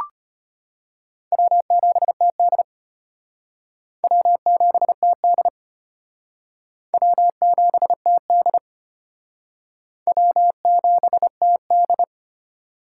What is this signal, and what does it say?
Morse code, heard as a keyed tone, about 33 wpm, 700 Hz. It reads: W7TD W7TD W7TD W7TD